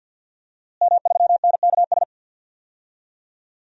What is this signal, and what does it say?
Morse code, heard as a keyed tone, about 51 wpm, 700 Hz. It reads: M3NKS
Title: DXCC-100 Call Signs - 5 Characters - Repeated 3x - 50wpm